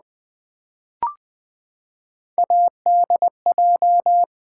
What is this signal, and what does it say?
Morse code, heard as a keyed tone, about 20 wpm, 700 Hz. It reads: ADJ
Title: Build A Word - 3 Letters At A Time - 8-Letter Words - 20wpm